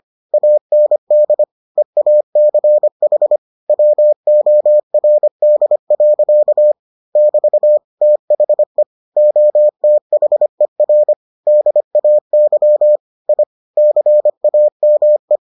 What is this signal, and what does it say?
Morse code, heard as a keyed tone, about 25 wpm, 600 Hz. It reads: AND EACH WORD. <BT> THE OTHER DAY I CAME